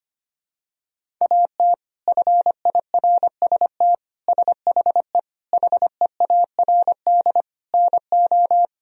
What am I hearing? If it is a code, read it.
Morse code, heard as a keyed tone, about 25 wpm, 700 Hz. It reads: AT FIRST SHE HEARD NO
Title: Sentences from Top 1000 Words - Review Part 1 - 25wpm